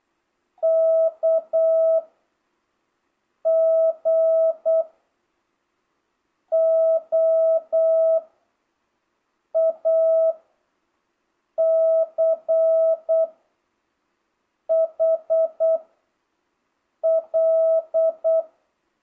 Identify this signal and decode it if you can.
Morse code, heard as a keyed tone, about 8 wpm, 660 Hz. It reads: K G O A C H L